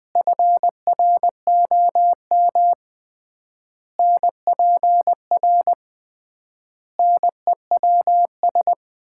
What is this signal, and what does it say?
Morse code, heard as a keyed tone, about 21 wpm, 700 Hz. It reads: FROM NPR NEWS
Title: 3x Word-Spacing - Sets of 5 Words - 20wpm